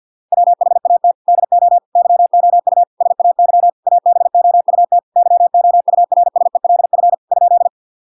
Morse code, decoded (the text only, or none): WHAT DO YOU SAY ABOUT YOURSELF ?